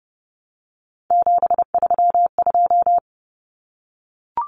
Morse code, heard as a keyed tone, about 30 wpm, 700 Hz. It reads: 732
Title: Numbers - 3-Digit 30wpm